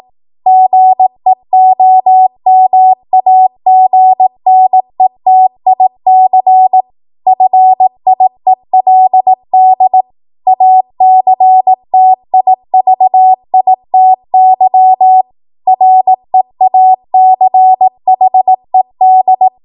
Morse code, heard as a keyed tone, about 18 wpm, 750 Hz. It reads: GEOMAGNETIC FIELD ACTIVITY REACHED
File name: Morse / 18WPM